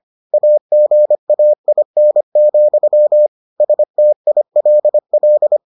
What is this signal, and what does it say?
Morse code, heard as a keyed tone, about 25 wpm, 600 Hz. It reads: AGAIN, STILL